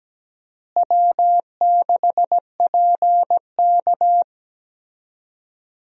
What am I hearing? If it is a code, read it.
Morse code, heard as a keyed tone, about 17 wpm, 700 Hz. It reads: W6PK
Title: USA Call Signs - 4 Characters - Repeated 2x - 17wpm